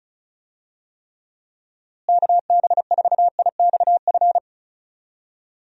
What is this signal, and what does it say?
Morse code, heard as a keyed tone, about 36 wpm, 700 Hz. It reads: KB4IXF
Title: USA Call Signs - 6 Characters - Repeated 3x - 35wpm